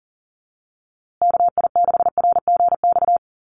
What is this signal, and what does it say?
Morse code, heard as a keyed tone, about 40 wpm, 700 Hz. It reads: KI6RGX
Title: USA Call Signs Repeated 2x 40wpm